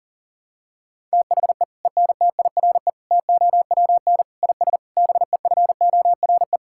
Morse code, heard as a keyed tone, about 40 wpm, 700 Hz. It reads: THE ENTIRE TOWN IS BEFORE
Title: Sentences from Top 1000 Words - Review Part 5 - 40wpm